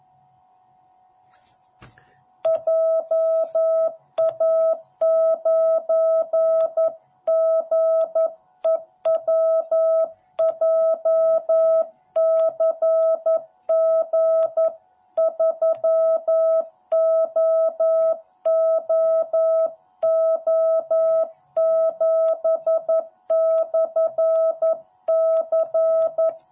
Morse code, heard as a keyed tone, about 11 wpm, 660 Hz. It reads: JA9GEW JCG3OOO7/C